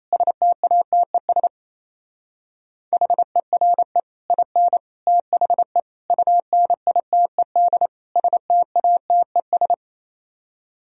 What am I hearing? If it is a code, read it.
Morse code, heard as a keyed tone, about 34 wpm, 700 Hz. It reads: STATES HERE IN THE UNITED STATES
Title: Speed-Racing - Sets of 5 Words 28wpm